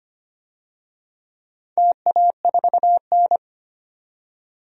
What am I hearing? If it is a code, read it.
Morse code, heard as a keyed tone, about 25 wpm, 700 Hz. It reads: TA4N
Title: DXCC-100 Call Signs - 4 Characters - 25wpm